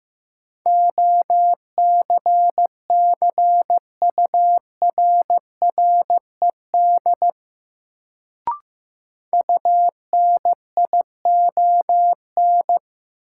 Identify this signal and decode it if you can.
Morse code, heard as a keyed tone, about 15 wpm, 700 Hz. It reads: OCCURRED UNION
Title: Sets of 5 Words Encore - New Word Drill - 15wpm